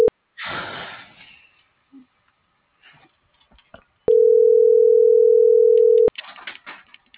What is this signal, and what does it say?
Ringback tone (North America), 440 + 480 Hz.